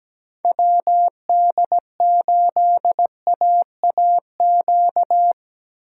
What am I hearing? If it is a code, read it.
Morse code, heard as a keyed tone, about 17 wpm, 700 Hz. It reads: WD8AAQ
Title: USA Call Signs - 2x Character Spacing - 17wpm